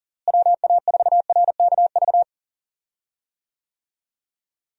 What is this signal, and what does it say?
Morse code, heard as a keyed tone, about 40 wpm, 700 Hz. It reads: WA4RKV
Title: USA Call Signs - 6 Characters - Repeated 3x - 40wpm